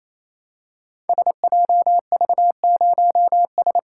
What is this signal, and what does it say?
Morse code, heard as a keyed tone, about 29 wpm, 700 Hz. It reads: SJV0S
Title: Code Groups - 5 Letter:Numbers - 28wpm